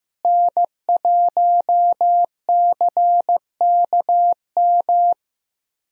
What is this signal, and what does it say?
Morse code, heard as a keyed tone, about 14 wpm, 700 Hz. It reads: N1CKM
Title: USA Call Signs - 5 Characters - Repeated 3x - 15wpm